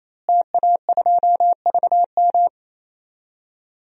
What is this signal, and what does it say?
Morse code, heard as a keyed tone, about 28 wpm, 700 Hz. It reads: TA2VM
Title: DXCC-100 Call Signs - 5 Characters - Repeated 3x - 28wpm